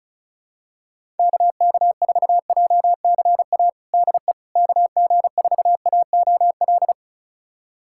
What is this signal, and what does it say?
Morse code, heard as a keyed tone, about 34 wpm, 700 Hz. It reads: KK4JCA DE KG4AOL
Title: USA Callsign DE Callsign - 6 Characters - 35wpm